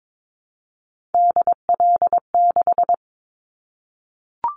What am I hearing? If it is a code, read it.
Morse code, heard as a keyed tone, about 22 wpm, 700 Hz. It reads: DL6
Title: Callsign Prefixes for Top 50 DXCC - 22wpm